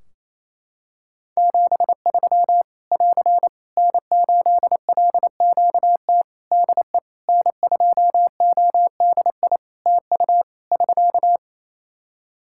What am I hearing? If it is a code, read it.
Morse code, heard as a keyed tone, about 28 wpm, 700 Hz. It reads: 73 <AR> N8LQT DE N2ODI TU <SK>
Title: K1USN SST QSO Practice 28wpm